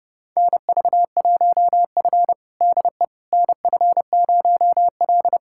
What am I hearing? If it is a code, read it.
Morse code, heard as a keyed tone, about 28 wpm, 700 Hz. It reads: NV1F DE NF0L
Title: USA Callsign DE Callsign - 4 Characters - 30wpm